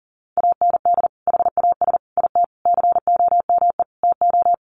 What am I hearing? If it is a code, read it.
Morse code, heard as a keyed tone, about 40 wpm, 700 Hz. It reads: AND HAS IT COME TO
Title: Sentences from Top 100 Words 40wpm